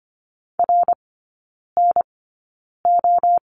Morse code, heard as a keyed tone, about 25 wpm, spaced as for 9 wpm, 700 Hz. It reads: RNO
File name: course / 25:10wpm / 093 Lesson 16e - Letter c - Three Characters 25wpm